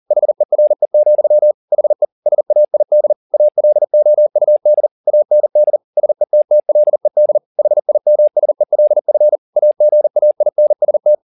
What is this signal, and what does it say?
Morse code, heard as a keyed tone, about 40 wpm, 600 Hz. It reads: HERE, HE SAID ALOUD AND SETTLED HIMSELF AGAINST